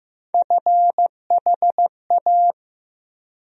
Morse code, heard as a keyed tone, about 16 wpm, 700 Hz. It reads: FHA
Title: USA Call Sign Postfixes - 3 Letters 15wpm